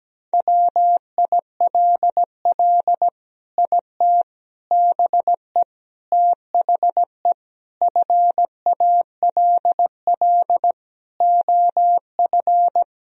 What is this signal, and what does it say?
Morse code, heard as a keyed tone, about 17 wpm, 700 Hz. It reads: WILL IT BE THE FALL OF
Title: Sentences from Top 1000 Words - Review Part 2 - 17wpm